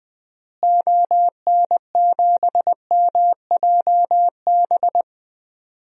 Morse code, heard as a keyed tone, about 21 wpm, 700 Hz. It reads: ON7MJB
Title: DXCC-100 Call Signs - 6 Characters - Repeated 3x - 20wpm